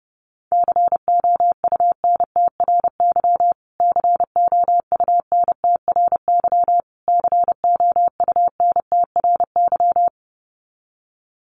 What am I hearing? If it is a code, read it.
Morse code, heard as a keyed tone, about 30 wpm, 700 Hz. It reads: COUNTRY COUNTRY COUNTRY
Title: Top 500 Words Repeated 30wpm